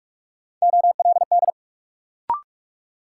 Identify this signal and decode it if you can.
Morse code, heard as a keyed tone, about 46 wpm, 700 Hz. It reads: OLD
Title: Sentences from Top 200 Words - New Word Drill - 45wpm